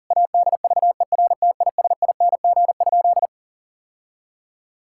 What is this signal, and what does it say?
Morse code, heard as a keyed tone, about 40 wpm, 700 Hz. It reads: ADVERTISING?
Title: Sentences from Top 1000 Words - Review Part 6 - 40wpm